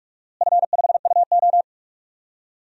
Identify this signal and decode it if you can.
Morse code, heard as a keyed tone, about 41 wpm, 700 Hz. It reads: F5UO
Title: DXCC-100 Call Signs - 4 Characters - Repeated 2x - 45wpm